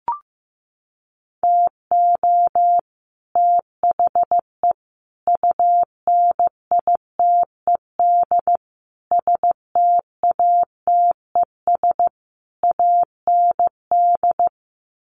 Morse code, heard as a keyed tone, about 15 wpm, 700 Hz. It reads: TO THE UNITED STATES AND
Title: Sets of 5 Words Encore 15wpm